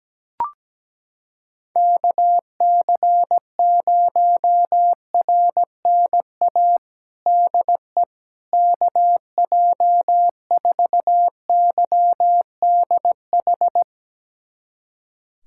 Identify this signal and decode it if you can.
Morse code, heard as a keyed tone, about 17 wpm, 700 Hz. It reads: KC0RNA DE KJ4YDH
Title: USA Callsign DE Callsign 17wpm